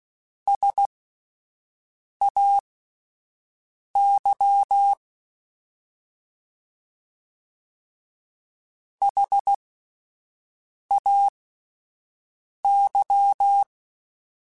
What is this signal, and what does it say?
Morse code, heard as a keyed tone, about 16 wpm, spaced as for 5 wpm, 800 Hz. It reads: SAY HAY
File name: S vs H / S vs H mp3s / S vs H @5